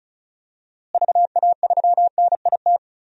Morse code, heard as a keyed tone, about 36 wpm, 700 Hz. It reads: VA3NIT
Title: DXCC-100 Call Signs - 6 Characters - Repeated 2x - 35wpm